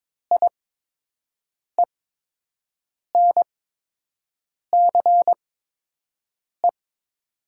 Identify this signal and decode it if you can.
Morse code, heard as a keyed tone, about 22 wpm, spaced as for 6 wpm, 700 Hz. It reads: IENCE
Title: 8x Character Spacing - Top 500 Words - 22wpm